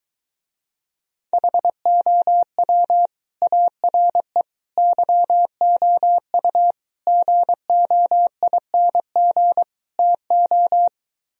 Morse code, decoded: HOW ARE YOU GOING TO

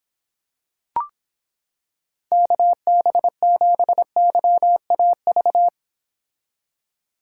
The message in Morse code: KB7YAV

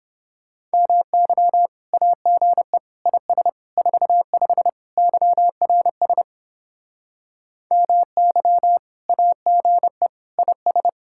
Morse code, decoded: MY AGE IS 45 YRS MY AGE IS

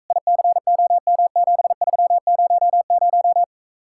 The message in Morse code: ICOM7300

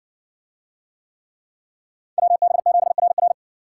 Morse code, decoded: WB7RR

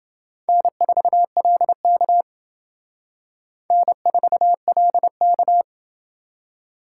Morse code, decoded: N4LK N4LK

